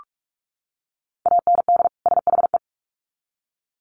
Morse code, heard as a keyed tone, about 44 wpm, 700 Hz. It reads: AND SHE
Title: Sets of 2 Words 45wpm